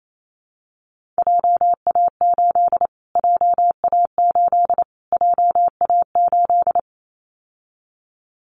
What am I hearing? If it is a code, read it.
Morse code, heard as a keyed tone, about 28 wpm, 700 Hz. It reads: JA8 JA8 JA8
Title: Callsign Prefixes for Top 25 DXCC - Repeated 3x - 28wpm